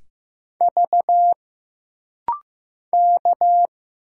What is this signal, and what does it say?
Morse code, heard as a keyed tone, about 15 wpm, 700 Hz. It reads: V K
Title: Single Letters - Rapid-Fire 15wpm